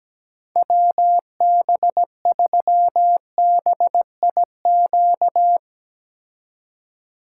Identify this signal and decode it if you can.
Morse code, heard as a keyed tone, about 17 wpm, 700 Hz. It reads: WB3BIQ
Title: K1USN SST QSO Practice - 17wpm